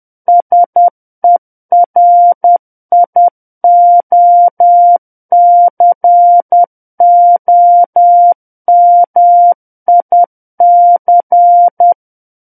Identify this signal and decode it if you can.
Morse code, heard as a keyed tone, about 10 wpm, 700 Hz. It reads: SERIOCOMIC
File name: Book One / 01-Chapter 1 - 10 WPM